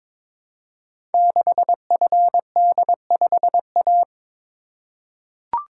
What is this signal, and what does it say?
Morse code, heard as a keyed tone, about 22 wpm, 700 Hz. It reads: 6FD5A
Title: Code Groups - 5 Letter:Numbers - 22wpm